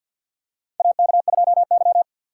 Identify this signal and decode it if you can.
Morse code, heard as a keyed tone, about 52 wpm, 700 Hz. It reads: AK2Y
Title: USA Call Signs - 4 Characters - 2x Character Spacing - 50wpm